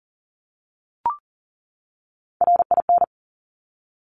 Morse code, heard as a keyed tone, about 42 wpm, 700 Hz. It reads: RIN
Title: Top Three Letter Combinations in Words 40wpm